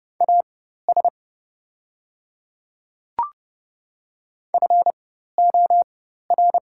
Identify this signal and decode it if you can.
Morse code, heard as a keyed tone, about 30 wpm, spaced as for 14 wpm, 700 Hz. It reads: AS FOR
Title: 4x Character Spacing - Top 100 Words - 30wpm